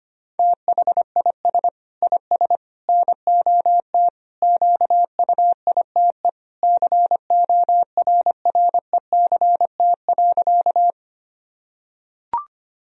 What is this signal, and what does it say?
Morse code, heard as a keyed tone, about 25 wpm, 700 Hz. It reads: THIS IS NOT QUITE CORRECT.
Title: Sentences from Top 1000 Words - Review Part 2 - 25wpm